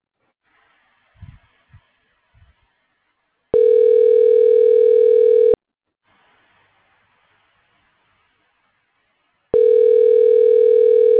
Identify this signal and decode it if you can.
Ringback tone (North America), 440 + 480 Hz.